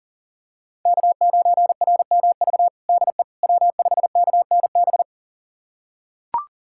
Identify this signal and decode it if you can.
Morse code, heard as a keyed tone, about 40 wpm, 700 Hz. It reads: K9RMV DE W5KNB
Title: USA Callsign DE Callsign - 5 Characters - 40wpm